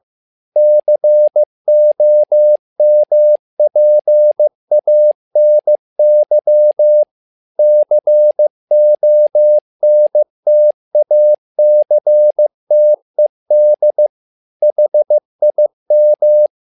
Morse code, heard as a keyed tone, about 15 wpm, 600 Hz. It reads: COMPANY CONTACTED HIM
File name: learning center / tower trouble / tower trouble 15wpm